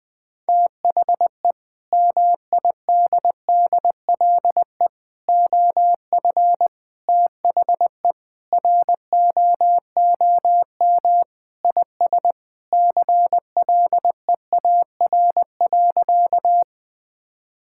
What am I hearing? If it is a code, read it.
Morse code, heard as a keyed tone, about 20 wpm, 700 Hz. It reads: THE MIDDLE OF THE ROOM IS CLEAR.
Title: Sentences from Top 1000 Words - Review Part 6 - 20wpm